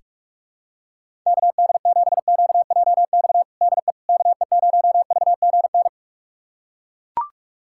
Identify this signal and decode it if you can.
Morse code, heard as a keyed tone, about 42 wpm, 700 Hz. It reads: KD7QJX DE KE0VGN
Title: USA Callsign DE Callsign 45wpm